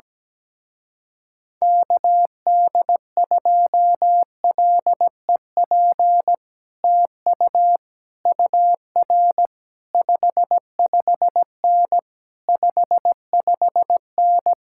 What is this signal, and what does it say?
Morse code, heard as a keyed tone, about 17 wpm, 700 Hz. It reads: KD2LEP TU UR 55N 55N
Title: Speed-Racing Common QSO Phrases - 17wpm